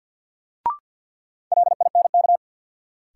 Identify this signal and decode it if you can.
Morse code, heard as a keyed tone, about 50 wpm, 700 Hz. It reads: LINK
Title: Sentences from Top 1000 Words - Review Part 7 - New Word Drill - 50wpm